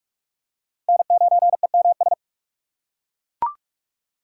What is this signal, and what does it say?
Morse code, heard as a keyed tone, about 41 wpm, 700 Hz. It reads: N9EMS
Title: USA Call Signs - 3x Character Spacing - 45wpm